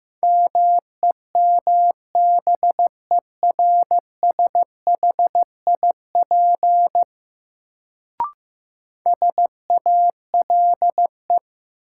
Morse code, heard as a keyed tone, about 16 wpm, 700 Hz. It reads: MEMBERSHIP SALE
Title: Sentences from Top 1400 Words - New Word Drill - 15wpm